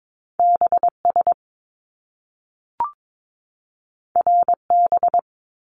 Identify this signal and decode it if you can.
Morse code, heard as a keyed tone, about 22 wpm, 700 Hz. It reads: BS RB
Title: Top Two Letter Combinations in Words 22wpm